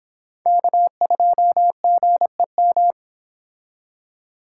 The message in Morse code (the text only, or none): K2GEM